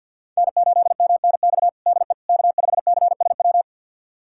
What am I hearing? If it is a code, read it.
Morse code, heard as a keyed tone, about 52 wpm, 700 Hz. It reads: N8MNX DE K5CSW
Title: USA Callsign DE Callsign 50wpm